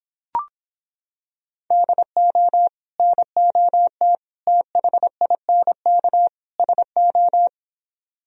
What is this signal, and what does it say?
Morse code, heard as a keyed tone, about 26 wpm, 700 Hz. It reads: DO NOT THINK SO